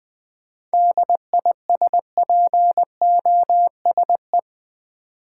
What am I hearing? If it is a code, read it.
Morse code, heard as a keyed tone, about 20 wpm, 690 Hz. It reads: DISPOSE
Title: 7 Letter Words - First 3 Letters Given - 20wpm